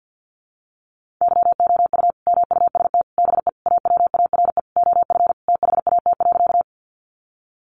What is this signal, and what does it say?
Morse code, heard as a keyed tone, about 50 wpm, 700 Hz. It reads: YOU MUST BE AWARE OF THAT.
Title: Sentences from Top 1200 Words 50wpm